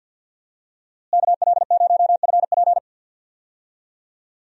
Morse code, heard as a keyed tone, about 53 wpm, 700 Hz. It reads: KL0FP
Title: USA Call Signs - 5 Characters - Repeated 2x - 50wpm